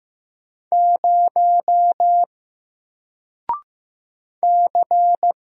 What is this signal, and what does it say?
Morse code, heard as a keyed tone, about 15 wpm, 700 Hz. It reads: 0 C
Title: Single Letter-Number 15wpm